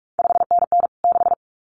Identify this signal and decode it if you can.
Morse code, heard as a keyed tone, about 46 wpm, 700 Hz. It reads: 5NN 6
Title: CQ WW DX Contest Exchange Encore - 45wpm